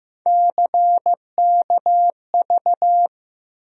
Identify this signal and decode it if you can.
Morse code, heard as a keyed tone, about 15 wpm, 700 Hz. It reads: CKV
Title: USA Call Sign Postfixes - 3 Letters 15wpm